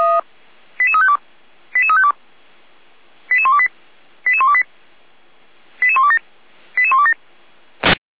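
Five-tone selective call (ZVEI), heard as quick runs of five tones, tones of about 70 ms. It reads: 88352 88352 88128 88128 88127 88127
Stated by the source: ZVEI five-tone selective call 88345 received on an SDR.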